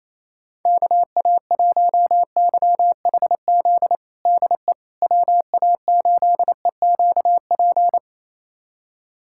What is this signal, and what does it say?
Morse code, heard as a keyed tone, about 28 wpm, 700 Hz. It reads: KA1YHZ DE WA8EQP
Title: USA Callsign DE Callsign - 6 Characters - 28wpm